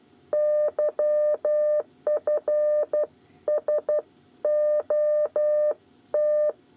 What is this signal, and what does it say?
Morse code, heard as a keyed tone, about 12 wpm, 600 Hz. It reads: YF S OT
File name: 8 bis lezione cw 2